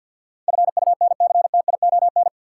Morse code, heard as a keyed tone, about 50 wpm, 700 Hz. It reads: FUNCTION
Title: Sentences from Top 1100 Words - New Word Drill - 50wpm